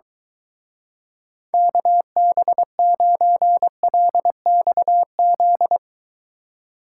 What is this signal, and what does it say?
Morse code, heard as a keyed tone, about 23 wpm, 700 Hz. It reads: KB9LXZ